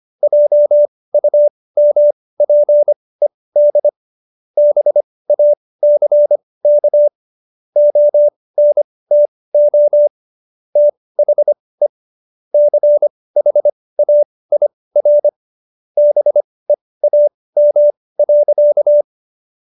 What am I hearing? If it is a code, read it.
Morse code, heard as a keyed tone, about 25 wpm, spaced as for 18 wpm, 590 Hz. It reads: JUMPED BACK ONTO THE CHAIR BEAM.